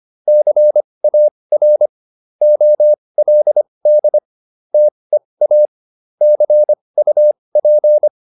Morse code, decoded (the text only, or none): CAR OLD TEA CUP